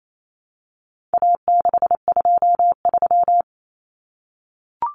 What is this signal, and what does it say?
Morse code, heard as a keyed tone, about 28 wpm, 700 Hz. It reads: A623
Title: Common Cut Numbers - 4-Digit 28wpm